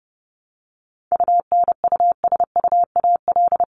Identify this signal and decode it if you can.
Morse code, heard as a keyed tone, about 30 wpm, 700 Hz. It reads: UNUSUAL
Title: Top 1251-1500 Words 30wpm